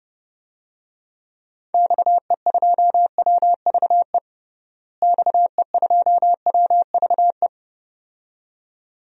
Morse code, heard as a keyed tone, about 30 wpm, 700 Hz. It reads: XE2WVE XE2WVE
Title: DXCC-100 Call Signs - 6 Characters - Repeated 2x - 30wpm